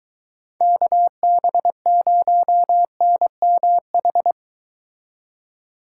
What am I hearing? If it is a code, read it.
Morse code, heard as a keyed tone, about 23 wpm, 700 Hz. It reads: KB0NMH